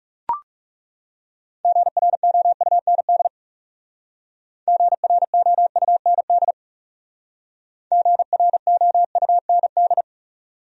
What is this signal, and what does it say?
Morse code, heard as a keyed tone, about 36 wpm, 700 Hz. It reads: GROUND GROUND GROUND
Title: Speed-Racing Top 500 Words 30wpm